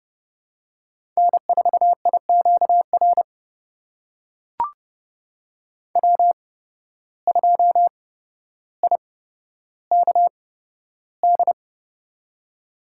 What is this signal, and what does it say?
Morse code, heard as a keyed tone, about 30 wpm, spaced as for 13 wpm, 700 Hz. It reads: N4IQR W2IKD
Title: USA Call Signs - 5 Characters - 8x Character Spacing - 30wpm